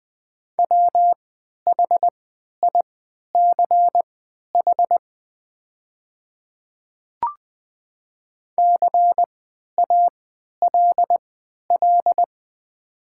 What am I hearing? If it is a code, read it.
Morse code, heard as a keyed tone, about 20 wpm, spaced as for 11 wpm, 700 Hz. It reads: WHICH CALL
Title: 3x Character Spacing - Top 200 Words - 20wpm